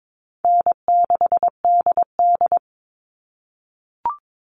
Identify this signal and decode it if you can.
Morse code, heard as a keyed tone, about 22 wpm, 700 Hz. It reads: N6DD
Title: All Cut Numbers - 4-Digit 22wpm